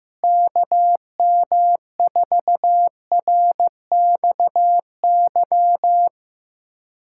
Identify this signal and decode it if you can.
Morse code, heard as a keyed tone, about 15 wpm, 700 Hz. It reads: KM4RXY
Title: USA Callsign DE Callsign 15wpm